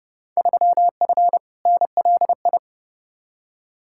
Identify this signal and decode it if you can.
Morse code, heard as a keyed tone, about 29 wpm, 700 Hz. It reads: 3F NLI
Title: Field Day Exchanges - 30wpm